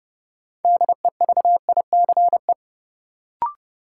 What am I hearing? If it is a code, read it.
Morse code, heard as a keyed tone, about 30 wpm, 700 Hz. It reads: DEVICE
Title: Sentences from Top 1500 Words - New Word Drill - 30wpm